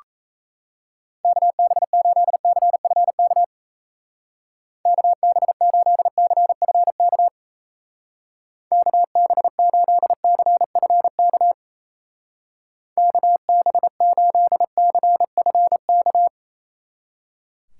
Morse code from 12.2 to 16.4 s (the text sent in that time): KB8CFK